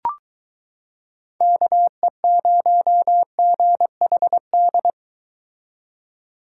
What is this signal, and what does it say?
Morse code, heard as a keyed tone, about 23 wpm, 700 Hz. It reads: KE0GHD